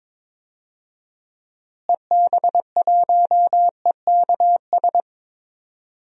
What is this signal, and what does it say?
Morse code, heard as a keyed tone, about 22 wpm, 700 Hz. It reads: EB1EKS
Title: DXCC-100 Call Signs - 6 Characters - Repeated 2x - 22wpm